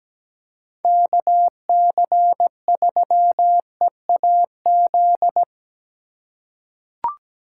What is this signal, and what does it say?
Morse code, heard as a keyed tone, about 17 wpm, 700 Hz. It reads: KC3EAZ
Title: USA Call Signs - 8x Character Spacing - 17wpm